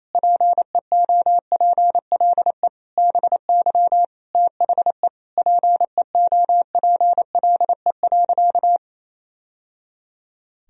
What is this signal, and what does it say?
Morse code, heard as a keyed tone, about 27 wpm, 700 Hz. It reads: PEOPLE BY THE PEOPLE.
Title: Sentences from Top 1000 Words - Review Part 3 - 28wpm